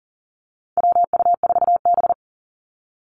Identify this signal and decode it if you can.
Morse code, heard as a keyed tone, about 40 wpm, 700 Hz. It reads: WU4B
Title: Call Signs - 40wpm